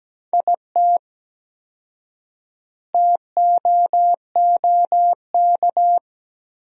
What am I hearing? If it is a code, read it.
Morse code, heard as a keyed tone, about 17 wpm, 700 Hz. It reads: IT TOOK
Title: 4x Word-Spacing - Sentences from Top 300 Words - 17wpm